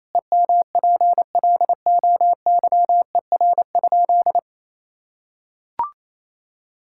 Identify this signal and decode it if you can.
Morse code, heard as a keyed tone, about 28 wpm, 700 Hz. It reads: EMPLOYER?
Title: Sentences from Top 1600 Words 28wpm